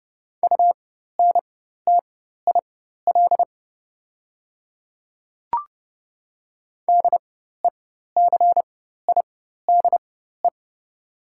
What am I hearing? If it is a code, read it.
Morse code, heard as a keyed tone, about 30 wpm, spaced as for 14 wpm, 700 Hz. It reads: UNTIL DECIDE
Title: 4x Character Spacing - Top 500 Words - 30wpm